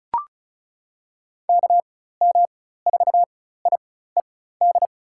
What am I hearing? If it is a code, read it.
Morse code, heard as a keyed tone, about 36 wpm, spaced as for 16 wpm, 700 Hz. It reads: KM4IED
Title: USA Call Signs - 6 Characters - 4x Character Spacing - 35wpm